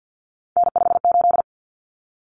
Morse code, heard as a keyed tone, about 50 wpm, 700 Hz. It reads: N58
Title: Common Cut Numbers - 3-Digit 50wpm